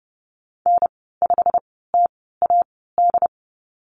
Morse code, heard as a keyed tone, about 30 wpm, spaced as for 17 wpm, 700 Hz. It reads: N5TAD
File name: Lesson 19g - Letter w - Call Signs 30wpm